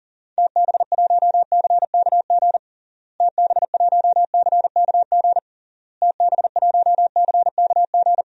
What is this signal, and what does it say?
Morse code, heard as a keyed tone, about 40 wpm, 700 Hz. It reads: TB1CKG TB1CKG TB1CKG
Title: DXCC-100 Call Signs - 6 Characters - Repeated 3x - 40wpm